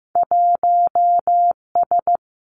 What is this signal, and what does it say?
Morse code, heard as a keyed tone, about 15 wpm, 700 Hz. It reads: 1 S
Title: Two Challenging Letter-Number Combinations 15wpm